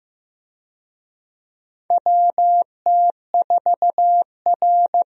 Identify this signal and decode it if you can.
Morse code, heard as a keyed tone, about 15 wpm, 700 Hz. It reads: WT4R
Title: USA Call Signs - 4 Characters - 15wpm